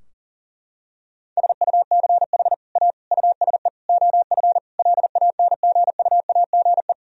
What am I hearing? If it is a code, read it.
Morse code, heard as a keyed tone, about 40 wpm, 700 Hz. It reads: SUCH A USE OF LANGUAGE
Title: Sentences from Top 1000 Words - Review Part 3 - 40wpm